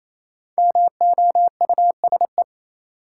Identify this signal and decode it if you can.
Morse code, heard as a keyed tone, about 28 wpm, 700 Hz. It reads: MOUSE
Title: Sentences from Top 700 Words - New Word Drill - 28wpm